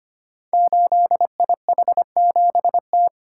Morse code, heard as a keyed tone, about 25 wpm, 700 Hz. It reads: 8IH7T
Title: Code Groups - 5 Letter:Numbers - 25wpm